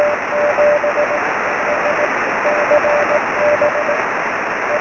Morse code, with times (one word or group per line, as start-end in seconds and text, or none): none